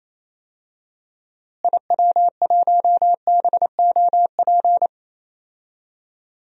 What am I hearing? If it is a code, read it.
Morse code, heard as a keyed tone, about 27 wpm, 700 Hz. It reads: IW1BOP
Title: DXCC-100 Call Signs - 6 Characters - Repeated 2x - 28wpm